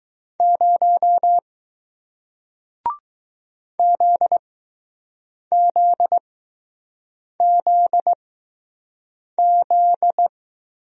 Morse code, heard as a keyed tone, about 20 wpm, 700 Hz. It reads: 0 Z Z Z Z